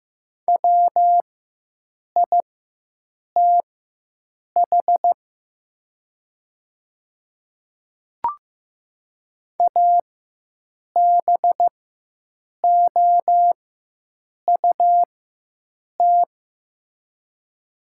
Morse code, heard as a keyed tone, about 15 wpm, spaced as for 7 wpm, 700 Hz. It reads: WITH ABOUT
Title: 4x Character Spacing - Top 100 Words - 15wpm